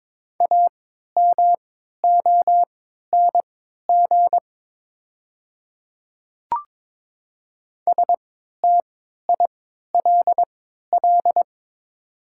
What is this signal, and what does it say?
Morse code, heard as a keyed tone, about 22 wpm, spaced as for 13 wpm, 700 Hz. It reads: AMONG STILL
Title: 3x Character Spacing - Top 500 Words - 22wpm